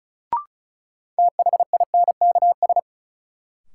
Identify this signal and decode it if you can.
Morse code, heard as a keyed tone, about 36 wpm, 700 Hz. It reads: THINKS
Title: Sentences from Top 1000 Words - Review Part 6 - New Word Drill - 35wpm